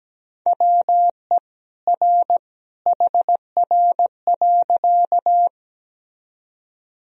Morse code, heard as a keyed tone, about 17 wpm, 700 Hz. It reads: WE R HR.
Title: Sentences from Top 200 Words and Abbreviations 17wpm